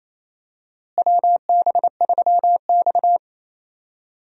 Morse code, heard as a keyed tone, about 28 wpm, 700 Hz. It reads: WB3X
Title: USA Call Signs - 4 Characters - Repeated 3x - 28wpm